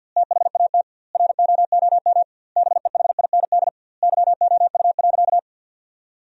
Morse code, heard as a keyed tone, about 53 wpm, 700 Hz. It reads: THAT ROOM BEHIND YOU.
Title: 2x Word-Spacing - Sentences from Top 500 Words - 50wpm